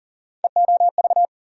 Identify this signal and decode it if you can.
Morse code, heard as a keyed tone, about 40 wpm, 700 Hz. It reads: EOV
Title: USA Call Sign Postfixes - 3 Letters 40wpm